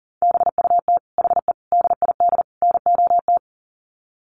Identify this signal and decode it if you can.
Morse code, heard as a keyed tone, about 40 wpm, 700 Hz. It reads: BUT HE DID NOT
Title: Sets of 4 Words 40wpm